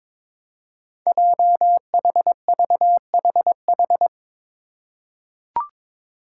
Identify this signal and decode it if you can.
Morse code, heard as a keyed tone, about 22 wpm, 700 Hz. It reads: JHVHH
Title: Five Challenging Letter-Number Combinations 22wpm